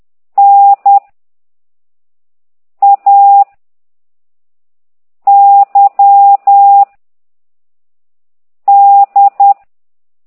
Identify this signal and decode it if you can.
Morse code, heard as a keyed tone, about 10 wpm, 800 Hz. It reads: N A Y D